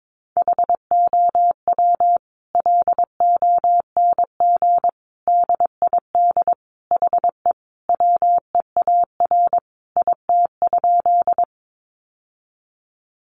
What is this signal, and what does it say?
Morse code, heard as a keyed tone, about 22 wpm, 700 Hz. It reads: HOW LONG DID HE WEAR IT?
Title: Sentences from Top 1500 Words 22wpm